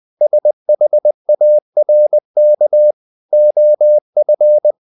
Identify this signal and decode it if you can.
Morse code, heard as a keyed tone, about 20 wpm, 600 Hz. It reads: SHARK OF
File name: the old man and the sea part 2 20wpm